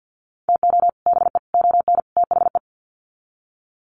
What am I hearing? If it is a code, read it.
Morse code, heard as a keyed tone, about 50 wpm, 700 Hz. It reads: TO BE ON THE
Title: Sets of 4 Words 50wpm